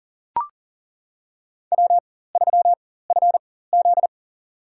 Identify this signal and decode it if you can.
Morse code, heard as a keyed tone, about 38 wpm, spaced as for 18 wpm, 700 Hz. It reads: W3FZ
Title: USA Call Signs - 4x Character Spacing - 40wpm